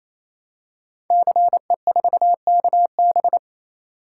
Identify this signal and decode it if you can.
Morse code, heard as a keyed tone, about 27 wpm, 700 Hz. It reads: CE4KB
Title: Code Groups - 5 Letter:Numbers - 28wpm